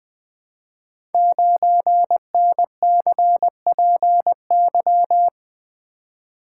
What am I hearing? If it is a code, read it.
Morse code, heard as a keyed tone, about 20 wpm, 700 Hz. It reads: 9NCPY
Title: Code Groups - 5 Letter:Numbers - 20wpm